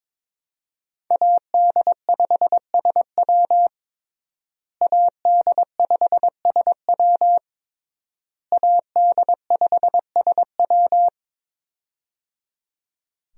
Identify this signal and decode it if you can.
Morse code, heard as a keyed tone, about 22 wpm, 700 Hz. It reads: AD5SW AD5SW AD5SW
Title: USA Call Signs - 5 Characters - Repeated 3x - 22wpm